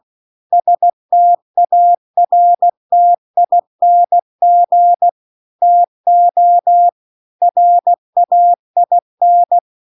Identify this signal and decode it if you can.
Morse code, heard as a keyed tone, about 16 wpm, 700 Hz. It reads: STARTING TO RAIN